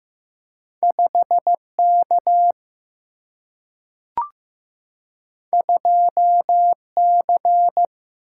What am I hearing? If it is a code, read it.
Morse code, heard as a keyed tone, about 15 wpm, 700 Hz. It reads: 5 K 2 C
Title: Single-Letter Single-Number 15wpm